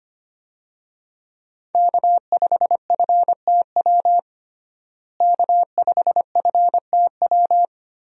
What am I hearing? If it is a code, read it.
Morse code, heard as a keyed tone, about 25 wpm, 700 Hz. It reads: K5FTW K5FTW
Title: USA Call Signs - 5 Characters - Repeated 3x - 25wpm